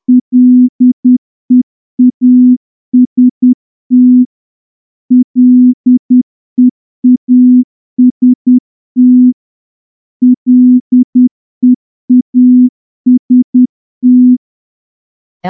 Morse code, heard as a keyed tone, about 10 wpm, 250 Hz. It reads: LEAST LEAST LEAST
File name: morse-training-10WPM